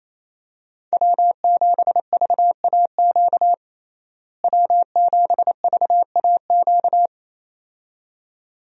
Morse code, heard as a keyed tone, about 28 wpm, 700 Hz. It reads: W7VAQ W7VAQ
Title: USA Call Signs - 5 Characters - Repeated 2x - 28wpm